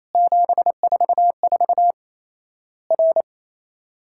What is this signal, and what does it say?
Morse code, heard as a keyed tone, about 28 wpm, 700 Hz. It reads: 744 R
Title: Worked All Europe DX Contest - QTC Exchange - 28wpm